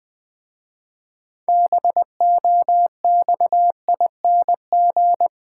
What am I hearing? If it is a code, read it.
Morse code, heard as a keyed tone, about 20 wpm, 690 Hz. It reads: BOXING
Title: Difficult Words with Hints - 20wpm